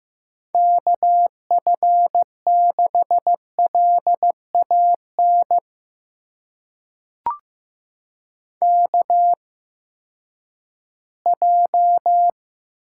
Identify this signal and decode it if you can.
Morse code, heard as a keyed tone, about 15 wpm, spaced as for 10 wpm, 700 Hz. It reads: KF6LAN KJ
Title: USA Call Signs - 6 Characters - 8x Character Spacing - 15wpm